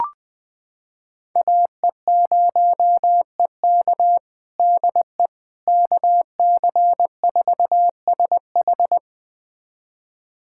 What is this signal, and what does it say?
Morse code, heard as a keyed tone, about 21 wpm, 700 Hz. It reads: AE0EK DE KC4SH
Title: USA Callsign DE Callsign - 5 Characters - 20wpm